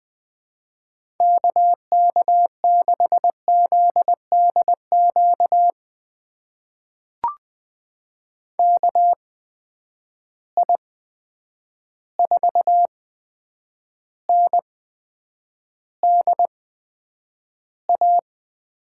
Morse code, heard as a keyed tone, about 20 wpm, spaced as for 9 wpm, 700 Hz. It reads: KK6ZDQ KI4NDA
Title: USA Call Signs - 6 Characters - 8x Character Spacing - 20wpm